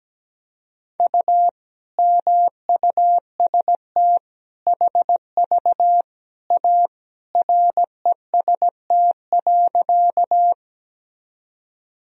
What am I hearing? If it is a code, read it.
Morse code, heard as a keyed tone, about 17 wpm, 700 Hz. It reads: U MUST HV A REST.
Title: Sentences from Top 500 Words and Abbreviations 17wpm